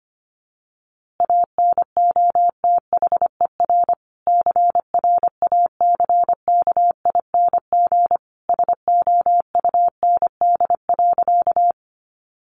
Morse code, heard as a keyed tone, about 25 wpm, 700 Hz. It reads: ANOTHER CRACKING SOUND.
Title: A Sound Of Thunder 25wpm